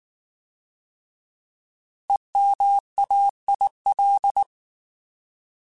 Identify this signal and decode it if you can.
Morse code, heard as a keyed tone, about 19 wpm, 800 Hz. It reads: EMAIL
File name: Alphabetical / Alphabetical mp3s / Es @18